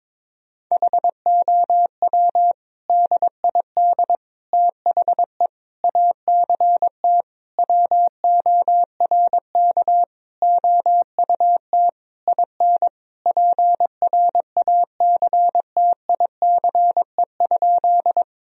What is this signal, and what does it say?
Morse code, heard as a keyed tone, about 22 wpm, 700 Hz. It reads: HOW DID THE ACT WORK OUT IN PRACTICE?
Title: Sentences from Top 1000 Words - Review Part 4 - 22wpm